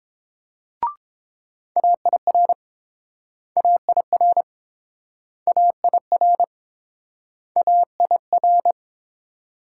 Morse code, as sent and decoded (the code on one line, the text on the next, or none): .- .. .-. / .- .. .-. / .- .. .-. / .- .. .-.
AIR AIR AIR AIR